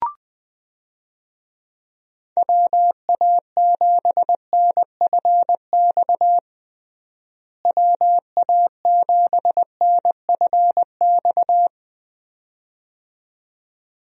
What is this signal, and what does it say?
Morse code, heard as a keyed tone, about 20 wpm, 700 Hz. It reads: WA7NFX WA7NFX
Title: USA Call Signs - 6 Characters - Repeated 2x - 20wpm